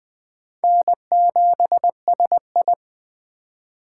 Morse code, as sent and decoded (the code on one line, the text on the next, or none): -. --... ... ..
N7SI